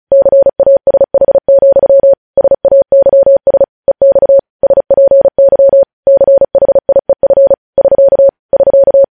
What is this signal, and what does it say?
Morse code, heard as a keyed tone, about 36 wpm, 550 Hz. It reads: CASH, SAYS EX SPY CHIEF <SK> <SK>
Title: News Headlines at 35